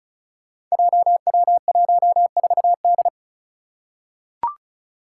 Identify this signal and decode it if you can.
Morse code, heard as a keyed tone, about 36 wpm, 700 Hz. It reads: JW14D
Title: Five Challenging Letter-Number Combinations 35wpm